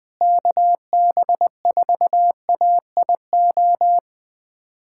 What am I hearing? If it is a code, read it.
Morse code, heard as a keyed tone, about 20 wpm, 700 Hz. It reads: KB4AIO
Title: USA Callsign DE Callsign - 6 Characters - 20wpm